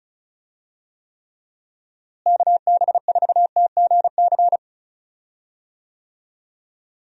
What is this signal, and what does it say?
Morse code, heard as a keyed tone, about 36 wpm, 700 Hz. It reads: KB4TGC
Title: USA Call Signs - 6 Characters - Repeated 2x - 35wpm